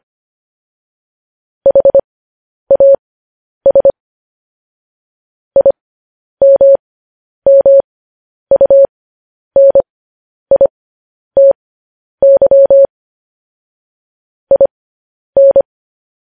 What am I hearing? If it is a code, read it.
Morse code, heard as a keyed tone, about 25 wpm, spaced as for 10 wpm, 550 Hz. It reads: HAS IMMUNITY IN